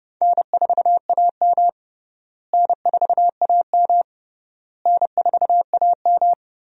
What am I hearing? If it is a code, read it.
Morse code, heard as a keyed tone, about 30 wpm, 700 Hz. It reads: N4AM N4AM N4AM
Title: USA Call Signs - 4 Characters - Repeated 3x - 30wpm